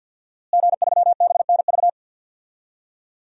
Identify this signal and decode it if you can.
Morse code, heard as a keyed tone, about 52 wpm, 700 Hz. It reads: G3BNV
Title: DXCC-100 Call Signs - 5 Characters - Repeated 3x - 50wpm